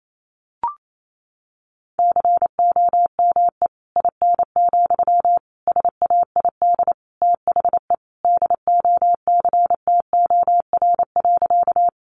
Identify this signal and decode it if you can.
Morse code, heard as a keyed tone, about 28 wpm, 700 Hz. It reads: COME IN, SAID THE DOCTOR.
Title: Sentences from Top 1500 Words 28wpm